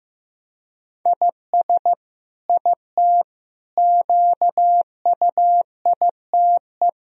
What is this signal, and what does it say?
Morse code, heard as a keyed tone, about 15 wpm, 700 Hz. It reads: IS IT QUITE
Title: Sentences from Top 1000 Words - Review Part 3 - 15wpm